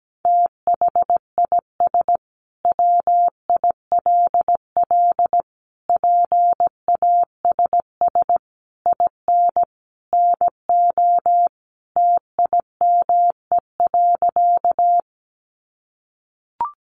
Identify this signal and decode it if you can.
Morse code, heard as a keyed tone, about 17 wpm, 700 Hz. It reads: THIS WILL PASS IN NO TIME.
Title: Sentences from Top 400 Words 17wpm